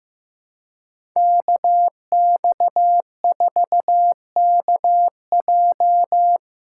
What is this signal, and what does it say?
Morse code, heard as a keyed tone, about 15 wpm, 700 Hz. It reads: KX4KJ
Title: USA Call Signs - 2x Character Spacing - 15wpm